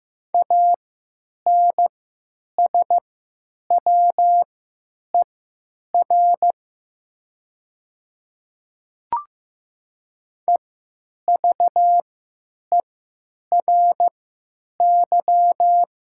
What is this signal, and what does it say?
Morse code, heard as a keyed tone, about 15 wpm, spaced as for 9 wpm, 700 Hz. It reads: ANSWER EVERY
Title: 3x Character Spacing - Top 300 Words - 15wpm